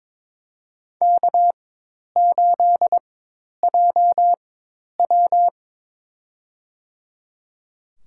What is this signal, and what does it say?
Morse code, heard as a keyed tone, about 21 wpm, spaced as for 10 wpm, 700 Hz. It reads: K8JW
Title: USA Call Signs - 4 Characters - 4x Character Spacing - 22wpm